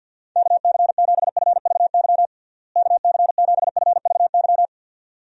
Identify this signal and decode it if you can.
Morse code, heard as a keyed tone, about 52 wpm, 700 Hz. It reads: KC7FVY KC7FVY
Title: USA Call Signs - 6 Characters - Repeated 2x - 50wpm